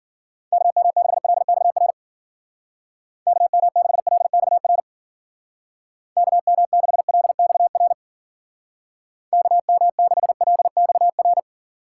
Morse code, heard as a keyed tone, about 49 wpm, 700 Hz. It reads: KM6LXR KM6LXR KM6LXR KM6LXR
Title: Speed Racing - USA 2x3 Call Signs 40wpm